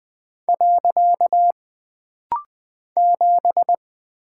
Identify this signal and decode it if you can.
Morse code, heard as a keyed tone, about 20 wpm, 700 Hz. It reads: . 7
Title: Single Character (Letter:Number:Punctuation:Prosign) - Rapid-Fire - 20wpm